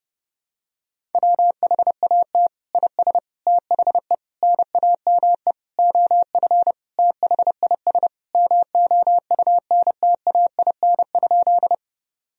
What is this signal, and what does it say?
Morse code, heard as a keyed tone, about 32 wpm, 700 Hz. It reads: WHAT IS THE NAME OF THIS MOUNTAIN?
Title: Sentences from Top 1000 Words - Review Part 1 - 30wpm